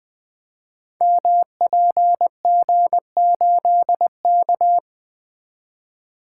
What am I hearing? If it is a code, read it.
Morse code, heard as a keyed tone, about 20 wpm, 700 Hz. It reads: MPG8K
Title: Code Groups - 5 Letter:Numbers - 20wpm